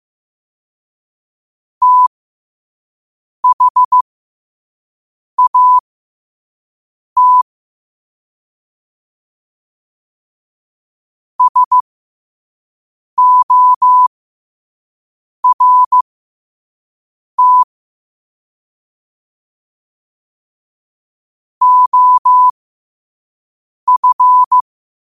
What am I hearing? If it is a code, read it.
Morse code, heard as a keyed tone, about 15 wpm, spaced as for 5 wpm, 1000 Hz. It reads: THAT SORT OF
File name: qotdfiles / 5wpm